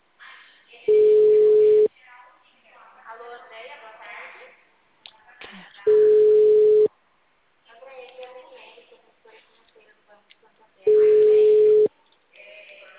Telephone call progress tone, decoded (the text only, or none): ringback tone